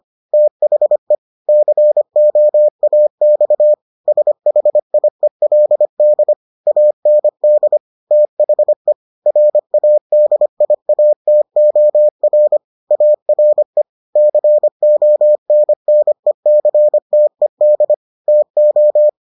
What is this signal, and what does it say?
Morse code, heard as a keyed tone, about 25 wpm, 600 Hz. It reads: THE COAX SHIELD AND THE RADIATOR ARE CONNECTED TO